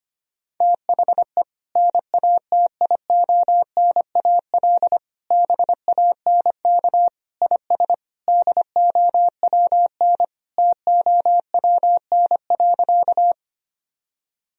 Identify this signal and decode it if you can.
Morse code, heard as a keyed tone, about 25 wpm, 700 Hz. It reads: THE NATIONAL BANK IS DOWN TOWN.
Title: Sentences from Top 1000 Words - Review Part 4 - 25wpm